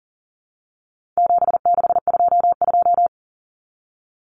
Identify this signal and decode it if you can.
Morse code, heard as a keyed tone, about 40 wpm, 700 Hz. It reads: 7622
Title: Common Cut Numbers - 4-Digit 40wpm